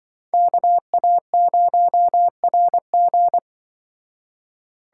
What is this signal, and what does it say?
Morse code, heard as a keyed tone, about 24 wpm, 700 Hz. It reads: KA0RG
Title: Speed Racing - USA 2x2 Call Signs 20wpm